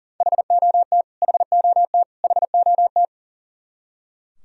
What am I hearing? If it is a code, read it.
Morse code, heard as a keyed tone, about 40 wpm, 700 Hz. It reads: HOT HOT HOT
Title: Top 100 Words Repeated 40wpm